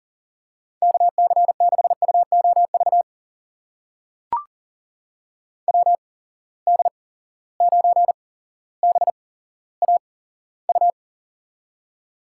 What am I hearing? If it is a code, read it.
Morse code, heard as a keyed tone, about 40 wpm, spaced as for 17 wpm, 700 Hz. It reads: KC6UOV WD9BAU
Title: USA Call Signs - 6 Characters - 8x Character Spacing - 40wpm